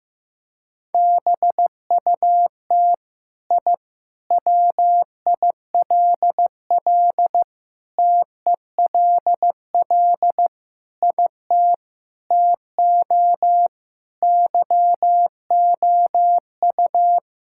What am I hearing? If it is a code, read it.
Morse code, heard as a keyed tone, about 15 wpm, 700 Hz. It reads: BUT I WILL TELL IT TO YOU
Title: Sentences from Top 1000 Words - Review Part 5 - 15wpm